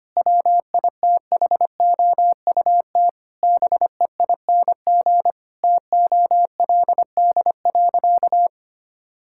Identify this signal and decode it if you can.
Morse code, heard as a keyed tone, about 25 wpm, 700 Hz. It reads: WITHOUT BEING TOLD.
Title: Sentences from Top 1000 Words - Review Part 6 - 25wpm